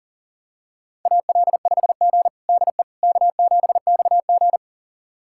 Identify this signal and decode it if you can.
Morse code, heard as a keyed tone, about 40 wpm, 700 Hz. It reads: AL5G DE K7XG
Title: USA Callsign DE Callsign - 4 Characters - 40wpm